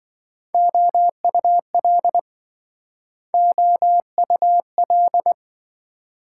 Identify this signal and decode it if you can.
Morse code, heard as a keyed tone, about 26 wpm, 700 Hz. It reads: OUL OUL